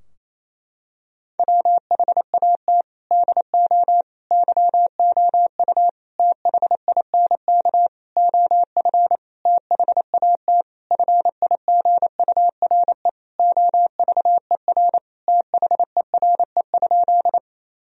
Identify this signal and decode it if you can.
Morse code, heard as a keyed tone, about 27 wpm, 700 Hz. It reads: WHAT DO YOU THINK OF THAT FIGURE OVER THERE?
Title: Sentences from Top 1000 Words - Review Part 2 - 28wpm